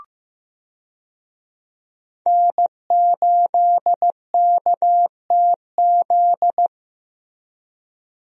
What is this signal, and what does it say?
Morse code, heard as a keyed tone, about 15 wpm, 700 Hz. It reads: N8KTZ
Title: USA Call Signs - 5 Characters - Repeated 2x - 15wpm